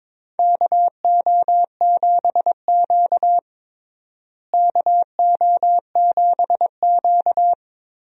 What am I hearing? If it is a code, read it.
Morse code, heard as a keyed tone, about 22 wpm, 700 Hz. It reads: KO7Q KO7Q
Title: USA Call Signs - 4 Characters - Repeated 2x - 22wpm